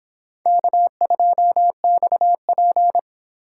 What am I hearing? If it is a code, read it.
Morse code, heard as a keyed tone, about 30 wpm, 700 Hz. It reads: K2XP